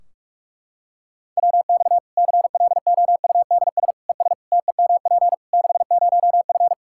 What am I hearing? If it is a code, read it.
Morse code, heard as a keyed tone, about 46 wpm, 700 Hz. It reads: WX CLOUDS ES TEMP 60F
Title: Practice QSO - Exchange 2 - Info - 45wpm